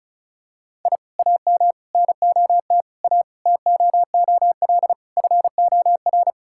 Morse code, heard as a keyed tone, about 34 wpm, 700 Hz. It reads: I AM NOT A TOOL FOR
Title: Sentences from Top 1000 Words - Review Part 6 - 35wpm